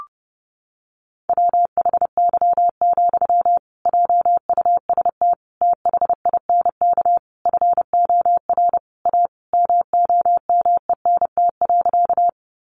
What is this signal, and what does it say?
Morse code, heard as a keyed tone, about 30 wpm, 700 Hz. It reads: WHY, JUST THINK FOR A MOMENT.
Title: Sentences from Top 1000 Words 30wpm